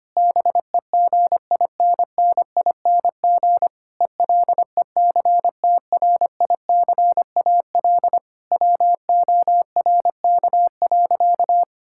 Morse code, heard as a keyed tone, about 25 wpm, 700 Hz. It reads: BEGINNING ELECTRICAL WORK.
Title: Sentences from Top 1000 Words - Review Part 6 - 25wpm